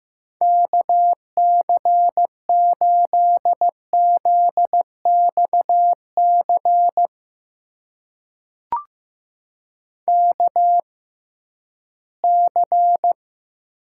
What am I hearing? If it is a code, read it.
Morse code, heard as a keyed tone, about 15 wpm, spaced as for 11 wpm, 700 Hz. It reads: KC8ZXC KC
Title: USA Call Signs - 6 Characters - 6x Character Spacing - 15wpm